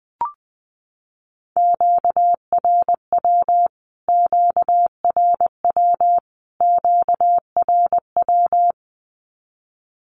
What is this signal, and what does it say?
Morse code, heard as a keyed tone, about 19 wpm, 700 Hz. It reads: QRW QRW QRW
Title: Q-Codes Repeated 3x 20wpm